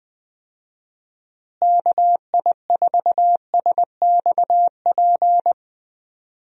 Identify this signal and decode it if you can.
Morse code, heard as a keyed tone, about 20 wpm, 700 Hz. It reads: KI4SXP
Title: USA Call Signs - 6 Characters - Repeated 3x - 20wpm